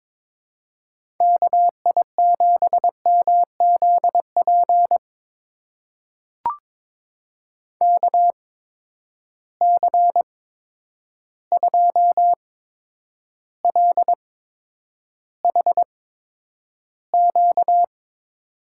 Morse code, heard as a keyed tone, about 22 wpm, spaced as for 9 wpm, 700 Hz. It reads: KI7MZP KC2LHQ
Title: USA Call Signs - 6 Characters - 8x Character Spacing - 22wpm